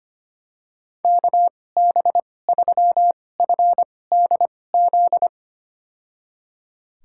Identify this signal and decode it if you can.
Morse code, heard as a keyed tone, about 25 wpm, spaced as for 18 wpm, 700 Hz. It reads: KB3FDZ
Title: USA Call Signs - 6 Characters - 2x Character Spacing - 25wpm